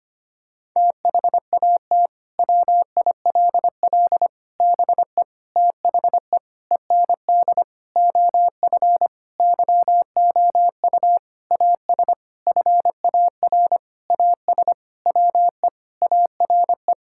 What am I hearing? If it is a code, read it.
Morse code, heard as a keyed tone, about 25 wpm, 700 Hz. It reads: THAT WILL BE THE END OF YOU AS FAR AS WE ARE
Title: Sentences from Top 1000 Words - Review Part 6 - 25wpm